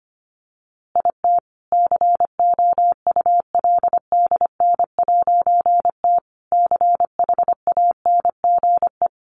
Morse code, heard as a keyed tone, about 25 wpm, 700 Hz. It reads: IT COULDN'T CHANGE
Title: A Sound Of Thunder 25wpm